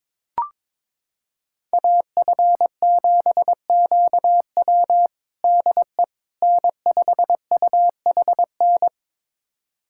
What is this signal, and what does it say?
Morse code, heard as a keyed tone, about 22 wpm, 700 Hz. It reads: AF7QW DE N5UHN
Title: USA Callsign DE Callsign - 5 Characters - 22wpm